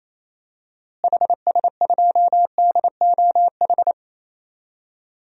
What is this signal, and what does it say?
Morse code, heard as a keyed tone, about 28 wpm, 700 Hz. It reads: HS2DOH
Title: DXCC-100 Call Signs - 6 Characters - Repeated 2x - 28wpm